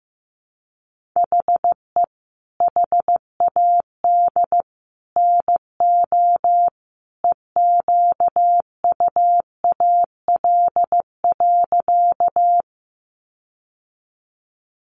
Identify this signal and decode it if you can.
Morse code, heard as a keyed tone, about 15 wpm, 700 Hz. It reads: HE HAD NO EQUAL.
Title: Sentences from Top 1400 Words 15wpm